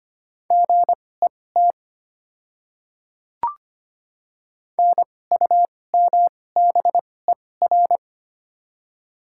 Morse code, heard as a keyed tone, about 25 wpm, spaced as for 18 wpm, 700 Hz. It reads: GET NUMBER
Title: 2x Character Spacing - Top 300 Words - 25wpm